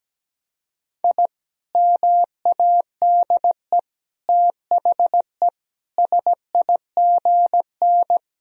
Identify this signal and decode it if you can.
Morse code, heard as a keyed tone, about 17 wpm, 700 Hz. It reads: I MADE THE SIGN
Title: Sentences from Top 1000 Words - Review Part 1 - 17wpm